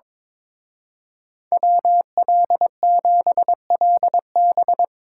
Morse code, heard as a keyed tone, about 24 wpm, 700 Hz. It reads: WL7LB
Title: Speed Racing - USA 2x2 Call Signs 22wpm